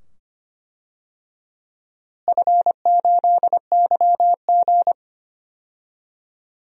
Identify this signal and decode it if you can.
Morse code, heard as a keyed tone, about 25 wpm, 700 Hz. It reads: F8YG
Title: DXCC-100 Call Signs - 4 Characters - Repeated 2x - 25wpm